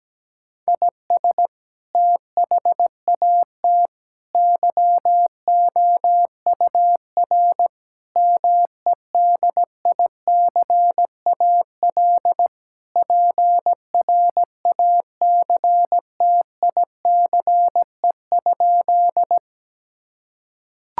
Morse code, heard as a keyed tone, about 17 wpm, 700 Hz. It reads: IS THAT YOUR MEDICAL PRACTICE?
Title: Sentences from Top 1000 Words - Review Part 5 - 17wpm